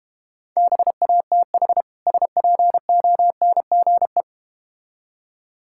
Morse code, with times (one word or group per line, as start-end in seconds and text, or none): 0.6-1.8 BATH
2.1-4.2 SPONGE